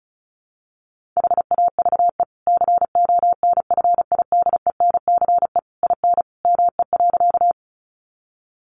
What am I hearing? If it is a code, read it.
Morse code, heard as a keyed tone, about 34 wpm, 700 Hz. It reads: HAVE CONFIDENCE IN ME.
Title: Sentences from Top 1500 Words 35wpm